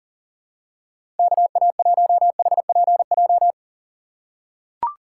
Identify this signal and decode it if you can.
Morse code, heard as a keyed tone, about 40 wpm, 700 Hz. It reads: KA1HPJ
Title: USA Call Signs - 6 Characters - 3x Character Spacing - 40wpm